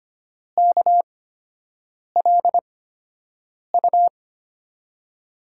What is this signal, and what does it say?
Morse code, heard as a keyed tone, about 25 wpm, spaced as for 7 wpm, 700 Hz. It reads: KLU
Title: USA Call Signs - 5 Characters - 8x Character Spacing - 25wpm